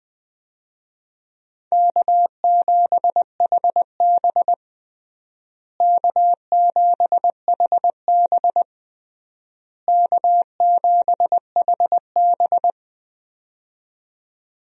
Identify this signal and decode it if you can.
Morse code, heard as a keyed tone, about 20 wpm, 700 Hz. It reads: K7HB K7HB K7HB
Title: USA Call Signs - 4 Characters - Repeated 3x - 20wpm